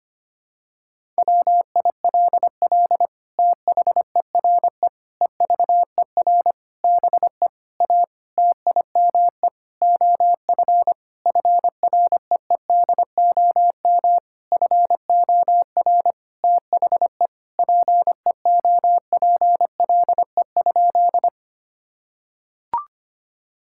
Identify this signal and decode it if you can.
Morse code, heard as a keyed tone, about 25 wpm, 700 Hz. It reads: WILL THERE EVER BE A TIME OF FREEDOM FOR THE PEOPLE?
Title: Sentences from Top 1000 Words - Review Part 5 - 25wpm